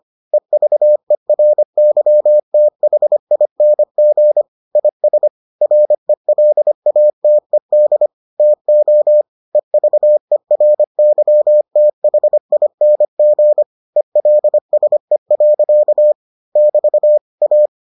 Morse code, heard as a keyed tone, about 25 wpm, 600 Hz. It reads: EVERYTHING IS RELATED TO EVERYTHING ELSE. <BT> A